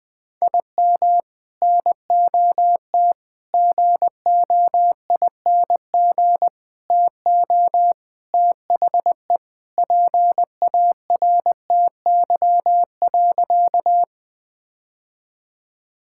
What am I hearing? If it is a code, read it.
Morse code, heard as a keyed tone, about 20 wpm, 700 Hz. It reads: IM NOT GOING TO THE PARTY.
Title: Sentences with 3-Word Phrases - 20wpm